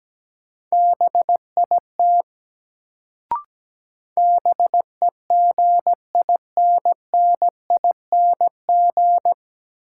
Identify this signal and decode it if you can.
Morse code, heard as a keyed tone, about 17 wpm, 700 Hz. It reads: BIT BEGINNING
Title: Sets of 3 Words - New Word Drill - 17wpm